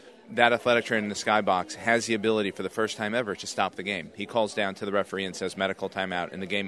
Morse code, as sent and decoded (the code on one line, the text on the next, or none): none